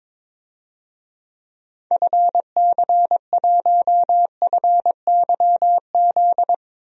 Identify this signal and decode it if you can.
Morse code, heard as a keyed tone, about 22 wpm, 700 Hz. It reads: FC1FYZ
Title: DXCC-100 Call Signs - 6 Characters - 22wpm